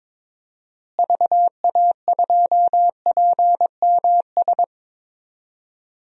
Morse code, heard as a keyed tone, about 22 wpm, 700 Hz. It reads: VA2PMS
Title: DXCC-100 Call Signs - 6 Characters - Repeated 2x - 22wpm